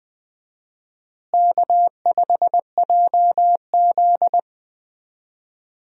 Morse code, heard as a keyed tone, about 20 wpm, 700 Hz. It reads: K5JZ
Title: USA Call Signs - 4 Characters - Repeated 2x - 20wpm